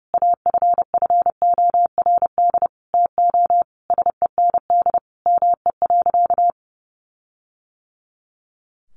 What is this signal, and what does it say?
Morse code, heard as a keyed tone, about 30 wpm, 700 Hz. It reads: AFFORD TO SEND ME.
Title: Sentences from Top 1500 Words 30wpm